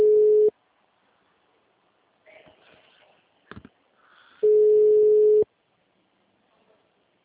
Ringback tone (Europe, 425 Hz).
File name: out- -2002- - - .487